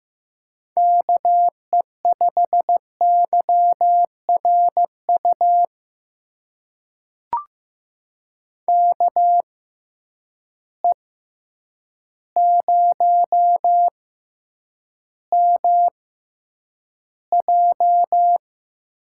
Morse code, heard as a keyed tone, about 15 wpm, spaced as for 8 wpm, 700 Hz. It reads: KE5YRU KE0MJ
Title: USA Call Signs - 6 Characters - 6x Character Spacing - 15wpm